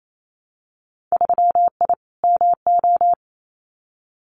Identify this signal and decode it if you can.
Morse code, heard as a keyed tone, about 28 wpm, 700 Hz. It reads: 3I MO
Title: Winter Field Day Exchanges 28wpm